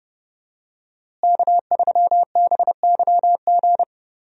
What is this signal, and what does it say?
Morse code, heard as a keyed tone, about 30 wpm, 700 Hz. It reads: K3BYG
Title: USA Call Signs - 5 Characters - Repeated 3x - 30wpm